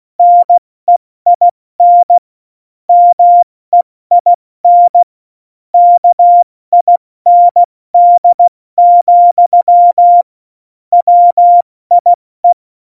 Morse code, heard as a keyed tone, about 16 wpm, spaced as for 14 wpm, 700 Hz. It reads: NEIN MEIN KIND, WIE
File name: Morsekurs / Witze / Witz 10. Oktober 17, Tempo 70